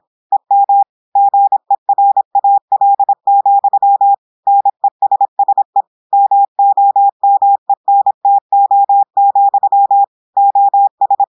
Morse code, heard as a keyed tone, about 26 wpm, 800 Hz. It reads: EM GERAL, NESSE MOMENTO, OS
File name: portugais texte en clair 2 vitesse 26 mots minutes